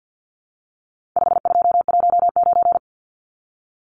Morse code, heard as a keyed tone, about 50 wpm, 700 Hz. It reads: 5219
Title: Numbers - 4-Digit 50wpm